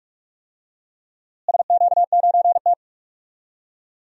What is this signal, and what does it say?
Morse code, heard as a keyed tone, about 46 wpm, 700 Hz. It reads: SQ9T
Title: DXCC-100 Call Signs - 4 Characters - Repeated 2x - 45wpm